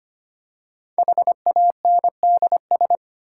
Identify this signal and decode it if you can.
Morse code, heard as a keyed tone, about 25 wpm, 700 Hz. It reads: HANDS
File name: practice / 25wpm / Easy Words with Hints - 25wpm